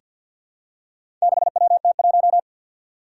Morse code, heard as a keyed tone, about 47 wpm, 700 Hz. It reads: 6WT1
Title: Four Challenging Letter-Number Combinations 50wpm